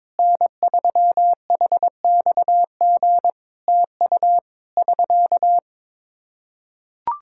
Morse code, heard as a keyed tone, about 22 wpm, 700 Hz. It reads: N3HXG TU <SK>
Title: Practice QSO - Exchange 4 - Ending - 22wpm